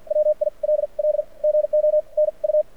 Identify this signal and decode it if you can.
Morse code, heard as a keyed tone, about 52 wpm, spaced as for 40 wpm, 600 Hz. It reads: WILL GO NU
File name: archived / CRSnet